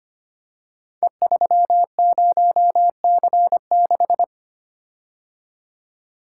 Morse code, heard as a keyed tone, about 25 wpm, 700 Hz. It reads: E30C6
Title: Code Groups - 5 Letter:Numbers - 25wpm